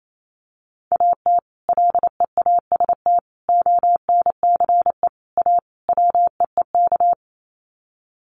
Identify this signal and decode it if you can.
Morse code, heard as a keyed tone, about 28 wpm, 700 Hz. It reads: AT LEAST ONCE A WEEK
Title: Sets of 5 Words Encore2 28wpm